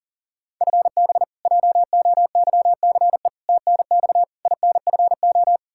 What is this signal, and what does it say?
Morse code, heard as a keyed tone, about 37 wpm, 700 Hz. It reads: FB JOYCE TNX INFO
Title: K1USN SST QSO Practice 40wpm